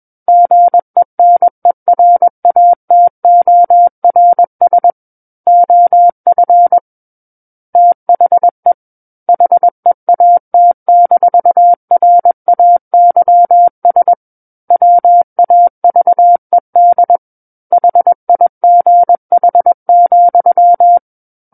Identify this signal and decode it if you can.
Morse code, heard as a keyed tone, about 21 wpm, 700 Hz. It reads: GENERATORS OF THE HEAT-RAYS WAVED HIGH,
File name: Book One / 12-Chapter 12 - 21 WPM